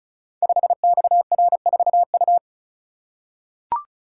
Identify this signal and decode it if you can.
Morse code, heard as a keyed tone, about 36 wpm, 700 Hz. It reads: 5XR4U
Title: Code Groups - 5 Letter:Numbers - 35wpm